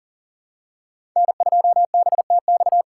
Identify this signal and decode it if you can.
Morse code, heard as a keyed tone, about 40 wpm, 700 Hz. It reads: N2BTX
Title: K1USN SST QSO Practice - 40wpm